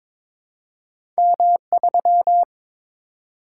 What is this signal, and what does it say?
Morse code, heard as a keyed tone, about 22 wpm, 700 Hz. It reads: M3
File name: practice / 22wpm / Two Letter-Number Combinations - 22wpm